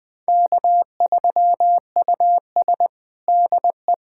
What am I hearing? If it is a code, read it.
Morse code, heard as a keyed tone, about 21 wpm, 700 Hz. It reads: K3US DE
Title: USA Callsign DE Callsign - 4 Characters - 20wpm